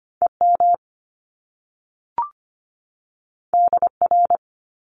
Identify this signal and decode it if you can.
Morse code, heard as a keyed tone, about 25 wpm, 700 Hz. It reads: EM DR
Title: Top Two Letter Combinations in Words 25wpm